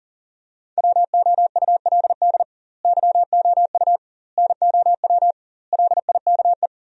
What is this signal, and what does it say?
Morse code, heard as a keyed tone, about 40 wpm, 700 Hz. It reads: WOULD YOU NOW LIKE
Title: 2x Word-Spacing - Sentences from Top 200 Words - 40wpm